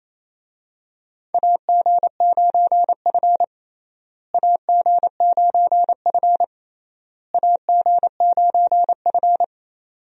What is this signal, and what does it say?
Morse code, heard as a keyed tone, about 28 wpm, 700 Hz. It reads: AG9F AG9F AG9F
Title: USA Call Signs - 4 Characters - Repeated 3x - 28wpm